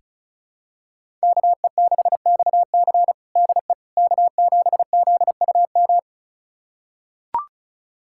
Morse code, heard as a keyed tone, about 34 wpm, 700 Hz. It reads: KE6XC DE K7ZUM
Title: USA Callsign DE Callsign - 5 Characters - 35wpm